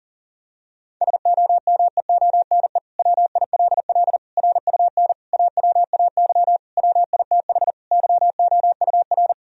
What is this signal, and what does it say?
Morse code, heard as a keyed tone, about 40 wpm, 700 Hz. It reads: SOMEONE WILL RUN AWAY WITH YOUR
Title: Sentences from Top 1000 Words - Review Part 1 - 40wpm